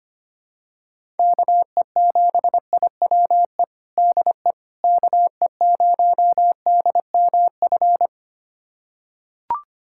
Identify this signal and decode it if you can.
Morse code, heard as a keyed tone, about 25 wpm, 700 Hz. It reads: KE7IWE DE KE0DMF
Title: USA Callsign DE Callsign 25wpm